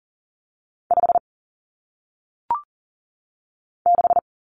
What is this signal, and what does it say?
Morse code, heard as a keyed tone, about 39 wpm, 700 Hz. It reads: 5 6
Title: Numbers - 1-Digit 40wpm